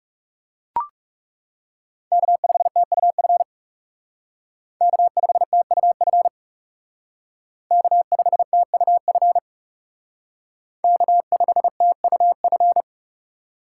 Morse code, heard as a keyed tone, about 40 wpm, 700 Hz. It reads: K5TUF K5TUF K5TUF K5TUF